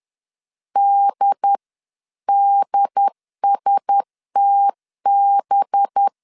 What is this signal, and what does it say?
Morse code, heard as a keyed tone, about 10 wpm, 800 Hz. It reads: DDSTB
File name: LEZIONI / 10 SIGNS / 10A SLASH 10wpm 2808